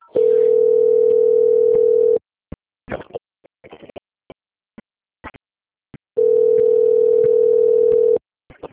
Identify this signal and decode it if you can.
Ringback tone (North America), 440 + 480 Hz.
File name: out- - - - - .15837